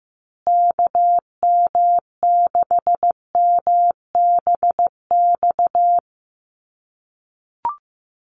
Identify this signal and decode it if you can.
Morse code, heard as a keyed tone, about 15 wpm, 700 Hz. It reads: KM6MBX
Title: USA Call Signs Repeated 3x 15wpm